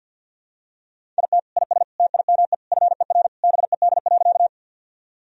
Morse code, heard as a keyed tone, about 49 wpm, 700 Hz. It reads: IT IS TIME FER BED.
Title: Sentences from Top 500 Words and Abbreviations 50wpm